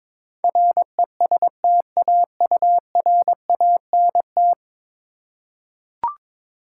Morse code, heard as a keyed tone, about 22 wpm, 700 Hz. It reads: RESTAURANT
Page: morse-code-ninja-practice-22wpm